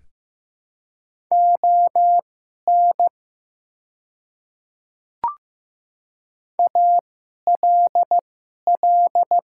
Morse code, heard as a keyed tone, about 15 wpm, spaced as for 11 wpm, 700 Hz. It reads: ON ALL
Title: 2x Character Spacing - Top 100 Words - 15wpm